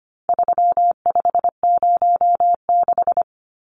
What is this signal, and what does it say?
Morse code, heard as a keyed tone, about 25 wpm, 700 Hz. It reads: 3506
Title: Numbers - 4-Digit 25wpm